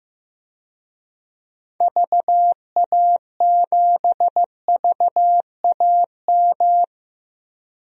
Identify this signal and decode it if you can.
Morse code, heard as a keyed tone, about 15 wpm, 700 Hz. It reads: VA7VAM
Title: DXCC-100 Call Signs - 6 Characters - Repeated 2x - 15wpm